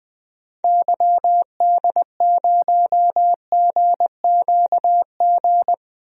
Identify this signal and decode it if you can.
Morse code, heard as a keyed tone, about 21 wpm, 700 Hz. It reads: YD0GQG
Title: DXCC-100 Call Signs - 6 Characters - 20wpm